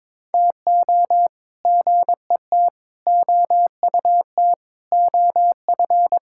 Morse code, heard as a keyed tone, about 22 wpm, 700 Hz. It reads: TO GET OUT OF
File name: practice / 22wpm / Sets of 8 Words - 22wpm